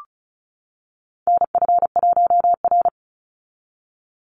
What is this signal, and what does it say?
Morse code, heard as a keyed tone, about 34 wpm, 700 Hz. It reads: NF1R
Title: NA SOTA Chaser Callsigns 35wpm